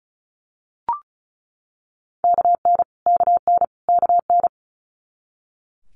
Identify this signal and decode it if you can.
Morse code, heard as a keyed tone, about 36 wpm, 700 Hz. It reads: KN KN KN
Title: Multi-letter Phonemes Repeated 3x 35wpm